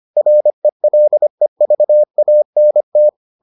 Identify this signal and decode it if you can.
Morse code, heard as a keyed tone, about 25 wpm, 600 Hz. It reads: RELEVANT